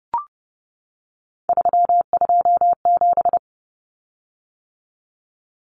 Morse code, heard as a keyed tone, about 30 wpm, 700 Hz. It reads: 327
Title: Common Cut Numbers - 3-Digit 30wpm